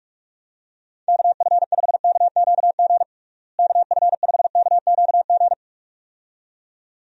Morse code, heard as a keyed tone, about 46 wpm, 700 Hz. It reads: KF5KQG KF5KQG
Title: USA Call Signs - 6 Characters - Repeated 2x - 45wpm